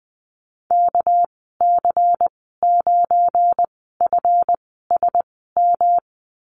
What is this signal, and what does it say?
Morse code, heard as a keyed tone, about 20 wpm, spaced as for 15 wpm, 700 Hz. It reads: KC9FSM
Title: 219 Lesson 35g - Number 0 - Call Signs 20wpm